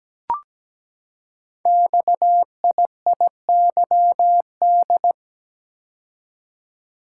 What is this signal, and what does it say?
Morse code, heard as a keyed tone, about 17 wpm, 700 Hz. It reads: XIIYD
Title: Code Groups - 5 Letter:Numbers - 17wpm